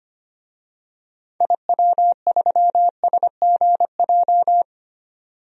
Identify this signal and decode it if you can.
Morse code, heard as a keyed tone, about 25 wpm, 700 Hz. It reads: IW3SGJ
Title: DXCC-100 Call Signs - 6 Characters - 25wpm